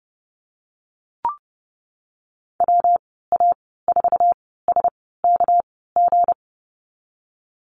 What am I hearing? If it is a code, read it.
Morse code, heard as a keyed tone, about 30 wpm, spaced as for 17 wpm, 700 Hz. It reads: WA4SKG
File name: Lesson 37g - Letter x - Call Signs 30wpm